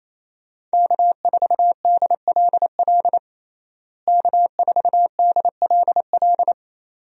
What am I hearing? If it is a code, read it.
Morse code, heard as a keyed tone, about 28 wpm, 700 Hz. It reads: K4DLL K4DLL
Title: USA Call Signs - 5 Characters - Repeated 2x - 28wpm